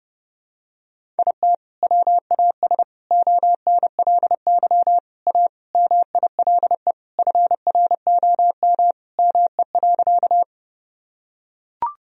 Morse code, heard as a keyed tone, about 32 wpm, 700 Hz. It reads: IT WAS ONLY A MILE FROM ME.
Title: Sentences from Top 1000 Words - Review Part 1 - 30wpm